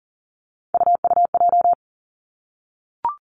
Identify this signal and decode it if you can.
Morse code, heard as a keyed tone, about 41 wpm, 690 Hz. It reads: UUJ
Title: Three Challenging Letter-Number Combinations 40wpm